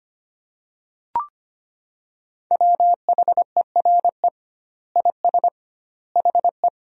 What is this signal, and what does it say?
Morse code, heard as a keyed tone, about 25 wpm, 700 Hz. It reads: WHERE IS HE
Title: 2x Word-Spacing - Sentences from Top 200 Words - 25wpm